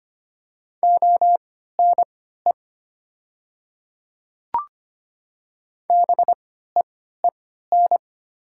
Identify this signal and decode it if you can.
Morse code, heard as a keyed tone, about 25 wpm, spaced as for 14 wpm, 700 Hz. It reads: ONE BEEN
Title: 3x Character Spacing - Top 100 Words - 25wpm